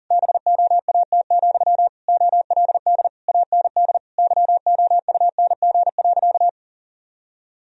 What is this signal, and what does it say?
Morse code, heard as a keyed tone, about 40 wpm, 700 Hz. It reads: BOAT, OLD AND YOUNG.
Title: Sentences from Top 1000 Words - Review Part 3 - 40wpm